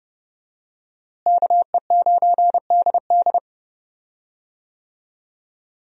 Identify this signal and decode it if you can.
Morse code, heard as a keyed tone, about 30 wpm, 700 Hz. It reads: KE9DD
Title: USA Call Signs - 5 Characters - Repeated 3x - 30wpm